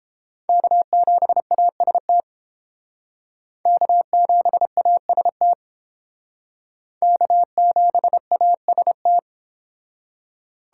Morse code, heard as a keyed tone, about 29 wpm, 700 Hz. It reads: K7AST K7AST K7AST